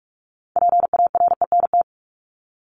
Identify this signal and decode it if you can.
Morse code, heard as a keyed tone, about 44 wpm, 700 Hz. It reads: PARENT
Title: Top 1751-2000 Words 45wpm